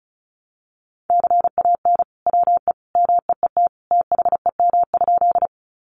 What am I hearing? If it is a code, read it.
Morse code, heard as a keyed tone, about 34 wpm, 700 Hz. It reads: CAN WE MEET THEM?
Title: Sentences from Top 1100 Words 35wpm